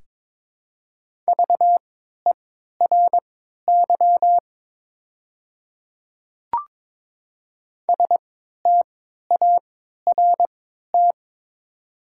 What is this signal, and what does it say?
Morse code, heard as a keyed tone, about 22 wpm, spaced as for 12 wpm, 690 Hz. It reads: VERY START
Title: 3x Character Spacing - Top 300 Words - 22wpm